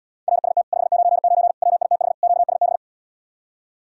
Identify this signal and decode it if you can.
Morse code, heard as a keyed tone, about 80 wpm, 700 Hz. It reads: RST 599 WITH QSB